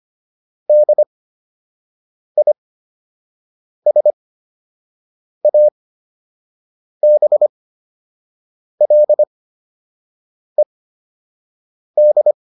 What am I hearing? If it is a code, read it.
Morse code, heard as a keyed tone, about 25 wpm, spaced as for 6 wpm, 600 Hz. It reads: DISABLED
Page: fundamental-practice-files